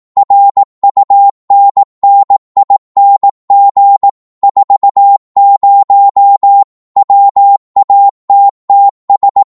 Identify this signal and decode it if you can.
Morse code, heard as a keyed tone, about 18 wpm, 800 Hz. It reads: RUNNING 40 WATTS
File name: Morse / qso-18wpm